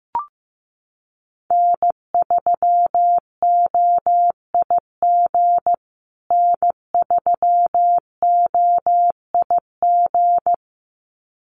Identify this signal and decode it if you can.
Morse code, heard as a keyed tone, about 15 wpm, 700 Hz. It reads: N3OIG N3OIG
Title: USA Call Signs Repeated 2x 15wpm